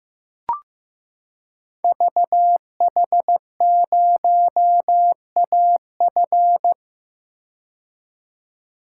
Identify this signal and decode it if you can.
Morse code, heard as a keyed tone, about 15 wpm, 700 Hz. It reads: VH0AF
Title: Code Groups - 5 Letter:Numbers - 15wpm